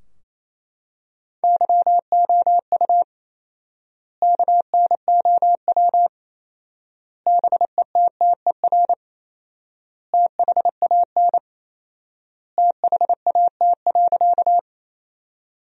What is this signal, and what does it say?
Morse code, heard as a keyed tone, about 28 wpm, 700 Hz. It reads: YOU KNOW BETTER THAN THAT.
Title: 4x Word-Spacing - Sentences from Top 400 Words - 28wpm